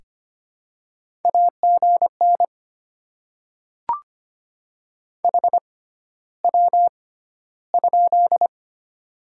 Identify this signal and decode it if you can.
Morse code, heard as a keyed tone, about 25 wpm, spaced as for 13 wpm, 700 Hz. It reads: AGN HW?
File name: Top 20 QSO Elements - 6x Character Spacing - 25wpm